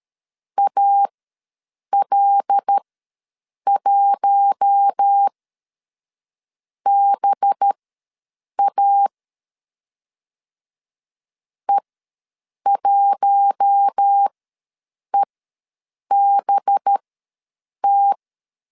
Morse code, heard as a keyed tone, about 13 wpm, spaced as for 7 wpm, 800 Hz. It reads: AL1BA E1EBT
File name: 5 LB ws 12wpm